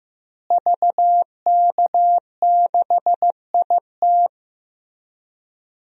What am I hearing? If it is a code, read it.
Morse code, heard as a keyed tone, about 15 wpm, 700 Hz. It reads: VK6IT
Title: DXCC-100 Call Signs - 5 Characters - Repeated 3x - 15wpm